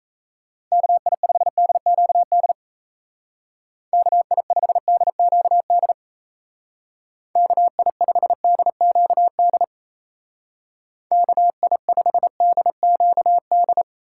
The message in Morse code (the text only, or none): KI5DQD KI5DQD KI5DQD KI5DQD